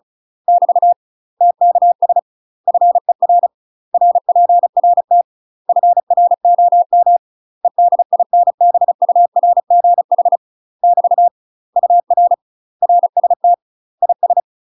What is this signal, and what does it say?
Morse code, heard as a keyed tone, about 34 wpm, 700 Hz. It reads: <BT> TKS FER RPRT FROM EDINBURGH <BT> UR RST IS
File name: UK MASTER QSOs TRAINING / 35WPM / UK MASTER QSOs 0005